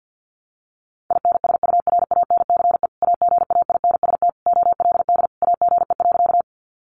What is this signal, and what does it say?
Morse code, heard as a keyed tone, about 50 wpm, 700 Hz. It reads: INSURANCE AGAINST OLD AGE.
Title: Sentences from Top 1100 Words 50wpm